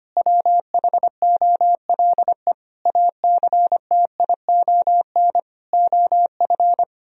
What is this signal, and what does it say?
Morse code, heard as a keyed tone, about 25 wpm, 700 Hz. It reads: WHOLE ACTION OF
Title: Sentences from Top 1000 Words - Review Part 4 - 25wpm